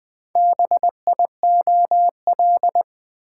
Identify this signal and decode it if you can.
Morse code, heard as a keyed tone, about 20 wpm, 700 Hz. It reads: BIOL
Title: Build A Word - 4 Letters At A Time - 10-Letter Words - 20wpm